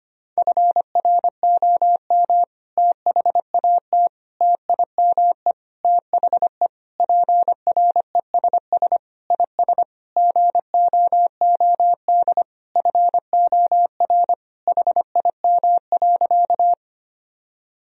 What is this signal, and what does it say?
Morse code, heard as a keyed tone, about 25 wpm, 700 Hz. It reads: FROM THAT TIME THE PRESS IS GOOD FOR HIM.
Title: Sentences from Top 1000 Words - Review Part 1 - 25wpm